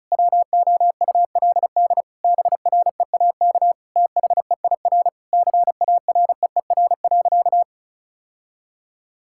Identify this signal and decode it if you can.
Morse code, heard as a keyed tone, about 36 wpm, 700 Hz. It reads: WOULD BREAK THEIR CAREER.
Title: Sentences from Top 1000 Words - Review Part 6 - 35wpm